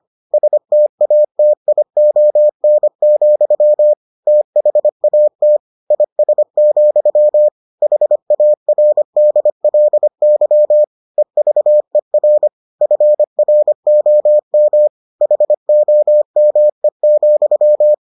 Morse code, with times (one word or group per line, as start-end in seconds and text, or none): 0.3-3.9 STATION,
4.3-5.6 THAT
5.9-7.5 IS,
7.8-10.8 HARDLY
11.2-12.5 EVER
12.8-14.9 FROM
15.2-17.9 HOME,